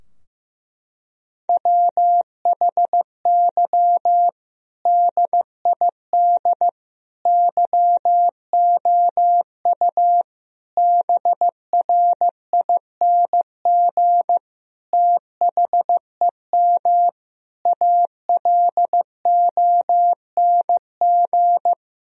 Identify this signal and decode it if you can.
Morse code, heard as a keyed tone, about 15 wpm, 700 Hz. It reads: WHY DID YOU BRING THEM ALONG
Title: Sentences from Top 1000 Words - Review Part 3 - 15wpm